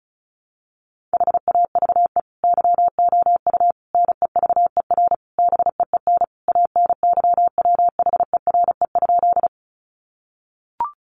Morse code, heard as a keyed tone, about 34 wpm, 700 Hz. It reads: HAVE YOU NEVER BEEN ANYWHERE?
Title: Sentences from Top 1400 Words 35wpm